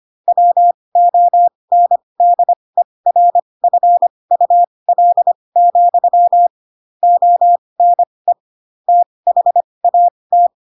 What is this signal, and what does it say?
Morse code, heard as a keyed tone, about 25 wpm, spaced as for 20 wpm, 700 Hz. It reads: WONDERFUL, ONE THAT
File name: Pinocchio / Pinocchio ch2 15 25 ik1hge